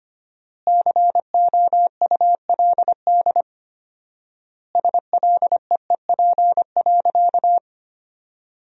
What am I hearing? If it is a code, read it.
Morse code, heard as a keyed tone, about 25 wpm, 700 Hz. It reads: COULD SLEEP.
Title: 4x Word-Spacing - Sentences from Top 500 Words - 25wpm